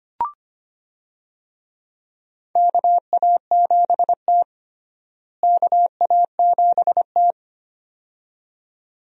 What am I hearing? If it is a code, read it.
Morse code, heard as a keyed tone, about 25 wpm, 700 Hz. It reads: KA7T KA7T
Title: USA Call Signs - 4 Characters - Repeated 2x - 25wpm